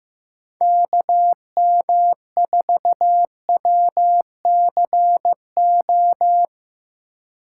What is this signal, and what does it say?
Morse code, heard as a keyed tone, about 17 wpm, 700 Hz. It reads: KM4WCO
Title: Speed Racing - USA 2x3 Call Signs 15wpm